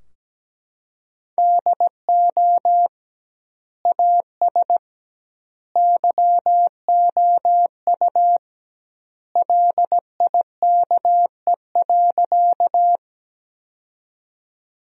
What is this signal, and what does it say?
Morse code, heard as a keyed tone, about 17 wpm, 700 Hz. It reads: DO AS YOU LIKE.
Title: 2x Word-Spacing - Sentences from Top 100 Words - 17wpm